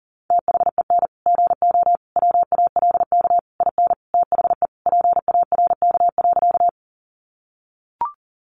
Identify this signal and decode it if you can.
Morse code, heard as a keyed tone, about 40 wpm, 700 Hz. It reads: THEN GO WALK IN THE PARK.
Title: Sentences from Top 1300 Words 40wpm